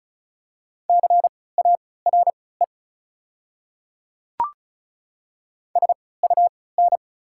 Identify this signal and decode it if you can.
Morse code, heard as a keyed tone, about 36 wpm, spaced as for 20 wpm, 700 Hz. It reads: CARE SUN
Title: 3x Character Spacing - Top 300 Words - 35wpm